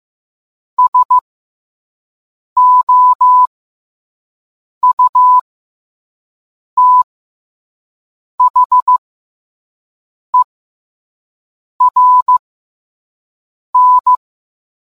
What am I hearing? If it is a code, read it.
Morse code, heard as a keyed tone, about 15 wpm, spaced as for 5 wpm, 1000 Hz. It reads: SOUTHERN